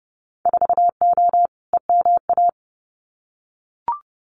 Morse code, heard as a keyed tone, about 30 wpm, 700 Hz. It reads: 4O EMA
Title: Winter Field Day Exchanges 30wpm